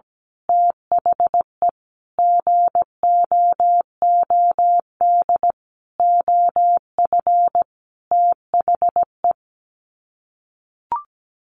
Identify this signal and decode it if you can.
Morse code, heard as a keyed tone, about 17 wpm, 700 Hz. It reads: THE GOOD OF THE
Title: Sets of 5 Words Encore2 17wpm